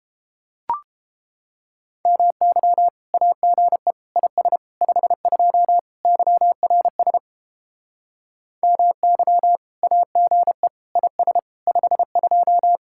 Morse code, heard as a keyed tone, about 29 wpm, 700 Hz. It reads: MY AGE IS 52 YRS MY AGE IS 52